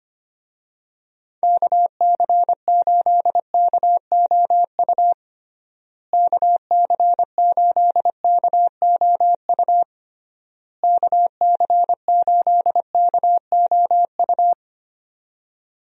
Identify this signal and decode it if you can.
Morse code, heard as a keyed tone, about 25 wpm, 700 Hz. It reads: KC8KOU KC8KOU KC8KOU
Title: USA Call Signs - 6 Characters - Repeated 3x - 25wpm